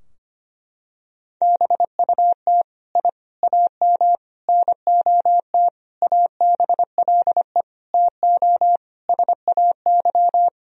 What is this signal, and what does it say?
Morse code, heard as a keyed tone, about 25 wpm, 700 Hz. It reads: BUT I AM NOT ABLE TO SAY
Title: Sentences from Top 1000 Words - Review Part 2 - 25wpm